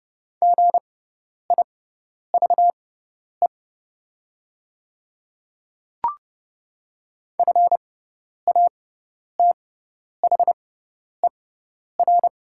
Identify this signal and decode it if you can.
Morse code, heard as a keyed tone, about 29 wpm, spaced as for 10 wpm, 700 Hz. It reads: GIVE FATHER
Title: 6x Character Spacing - Top 200 Words - 30wpm